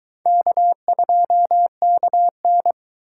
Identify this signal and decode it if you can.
Morse code, heard as a keyed tone, about 22 wpm, 700 Hz. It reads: K2KN